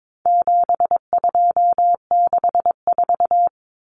Morse code, heard as a keyed tone, about 22 wpm, 700 Hz. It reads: 7264
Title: Numbers - 4-Digit 22wpm